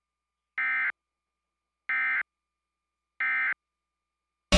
Emergency Alert System (SAME) tones, heard as harsh buzzing bursts. It is heard from 0.6 to 3.5 s.